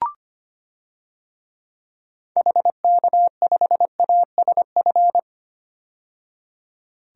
Morse code, heard as a keyed tone, about 25 wpm, 700 Hz. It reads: HK5ASF
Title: DXCC-100 Call Signs - 6 Characters - 25wpm